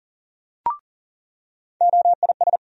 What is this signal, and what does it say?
Morse code, heard as a keyed tone, about 40 wpm, 700 Hz. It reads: OIS
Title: USA Call Sign Postfixes - 3 Letters 40wpm